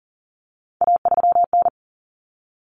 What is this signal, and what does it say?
Morse code, heard as a keyed tone, about 40 wpm, 700 Hz. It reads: A3N
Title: Common Cut Numbers - 3-Digit 40wpm